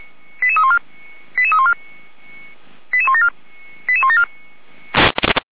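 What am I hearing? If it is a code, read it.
Five-tone selective call (ZVEI), heard as quick runs of five tones, tones of about 70 ms. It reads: 88314 88314 88174 88174
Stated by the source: ZVEI five-tone call received by SDR.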